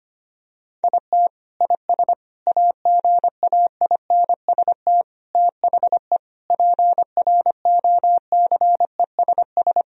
Morse code, heard as a keyed tone, about 25 wpm, 700 Hz. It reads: IT IS AGAINST THE PROCESS
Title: Sentences from Top 1000 Words - Review Part 3 - 25wpm